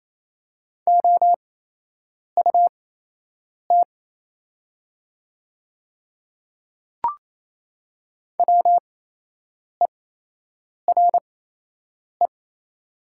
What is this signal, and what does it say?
Morse code, heard as a keyed tone, about 28 wpm, spaced as for 8 wpm, 700 Hz. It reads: OUT WERE
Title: 8x Character Spacing - Top 200 Words - 28wpm